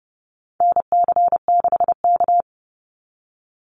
Morse code, heard as a keyed tone, about 29 wpm, 700 Hz. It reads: NC6K
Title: Call Signs - 30wpm